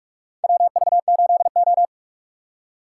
Morse code, heard as a keyed tone, about 44 wpm, 690 Hz. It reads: WV8O
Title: USA Call Signs - 4 Characters - 4x Character Spacing - 45wpm